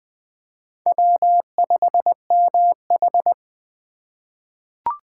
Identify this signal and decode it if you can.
Morse code, heard as a keyed tone, about 20 wpm, 700 Hz. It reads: W5MH
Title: USA Call Signs - 4 Characters - 6x Character Spacing - 20wpm